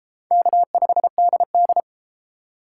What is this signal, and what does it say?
Morse code, heard as a keyed tone, about 31 wpm, 700 Hz. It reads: K5DD